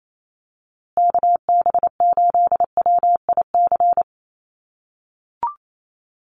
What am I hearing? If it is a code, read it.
Morse code, heard as a keyed tone, about 28 wpm, 700 Hz. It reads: KB8WIC
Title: USA Call Signs Repeated 2x 28wpm